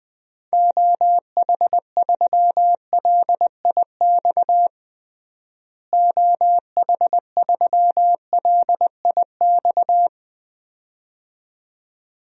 Morse code, heard as a keyed tone, about 20 wpm, 700 Hz. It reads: OH3LIX OH3LIX
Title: DXCC-100 Call Signs - 6 Characters - Repeated 2x - 20wpm